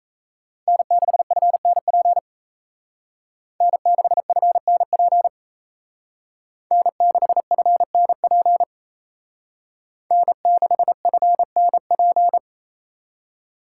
Morse code, heard as a keyed tone, about 35 wpm, 700 Hz. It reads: N6FNP N6FNP N6FNP N6FNP